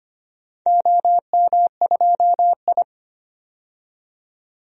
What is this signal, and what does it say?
Morse code, heard as a keyed tone, about 25 wpm, 700 Hz. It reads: OM2I
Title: DXCC-100 Call Signs - 4 Characters - 25wpm